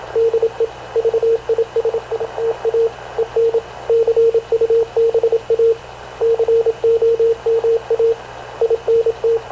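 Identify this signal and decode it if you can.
Morse code, heard as a keyed tone, about 26 wpm, 460 Hz, with radio noise. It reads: DE VISITA EN CUBA COMA INT
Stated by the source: Havana Radio / CLA sending Spanish press (PX) in Morse, recorded by Radio Officer Paul Dumesnil onboard CNAV QUEST/CZDO.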